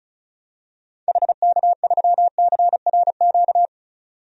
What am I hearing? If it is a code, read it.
Morse code, heard as a keyed tone, about 36 wpm, 690 Hz. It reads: HK3CRQ
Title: DXCC-100 Call Signs - 6 Characters - 35wpm